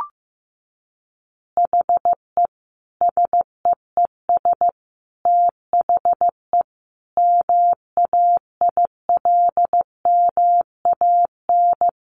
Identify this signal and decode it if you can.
Morse code, heard as a keyed tone, about 14 wpm, 700 Hz. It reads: HE SEES THE MAILMAN
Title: Easy Phrases 15wpm